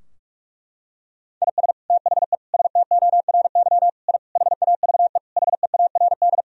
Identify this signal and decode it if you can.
Morse code, heard as a keyed tone, about 43 wpm, 700 Hz. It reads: IS THE STORY I HAVE HEARD
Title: Sentences from Top 1000 Words - Review Part 2 - 45wpm